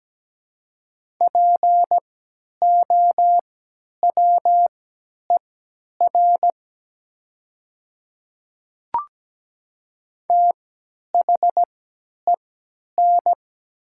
Morse code, heard as a keyed tone, about 17 wpm, spaced as for 10 wpm, 700 Hz. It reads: POWER THEN
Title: 3x Character Spacing - Top 500 Words - 17wpm